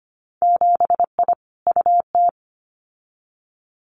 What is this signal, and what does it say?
Morse code, heard as a keyed tone, about 25 wpm, 700 Hz. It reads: 7I UT
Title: Winter Field Day Exchanges Encore 25wpm